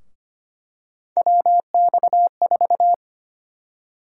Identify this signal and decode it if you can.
Morse code, heard as a keyed tone, about 25 wpm, 700 Hz. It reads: WX4
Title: USA Call Sign Prefixes - Repeated 2x - 25wpm